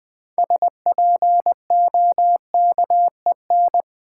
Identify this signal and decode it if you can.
Morse code, heard as a keyed tone, about 20 wpm, 700 Hz. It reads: SPOKEN
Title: Build A Word - 2 Letters At A Time - 6-Letter Words - 20wpm